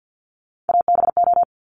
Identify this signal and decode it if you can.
Morse code, heard as a keyed tone, about 50 wpm, 700 Hz. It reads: ABO
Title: Top Three Letter Combinations in Words 50wpm